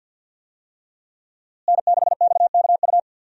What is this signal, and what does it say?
Morse code, heard as a keyed tone, about 50 wpm, 700 Hz. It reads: N6XKU
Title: USA Call Signs - 5 Characters - Repeated 3x - 50wpm